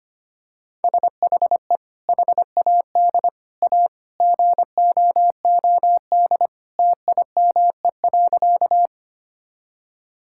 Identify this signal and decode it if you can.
Morse code, heard as a keyed tone, about 25 wpm, 690 Hz. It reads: SHE HAD A GOOD TIME.
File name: practice / 25wpm / Sentences with 2-Word Phrases - 25wpm